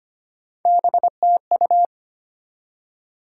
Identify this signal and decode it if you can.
Morse code, heard as a keyed tone, about 33 wpm, 700 Hz. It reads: BTU